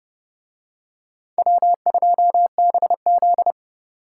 Morse code, heard as a keyed tone, about 27 wpm, 700 Hz. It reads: W2BZ